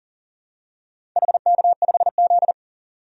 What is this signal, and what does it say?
Morse code, heard as a keyed tone, about 40 wpm, 700 Hz. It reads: HK5Z
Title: DXCC-100 Call Signs - 4 Characters - 40wpm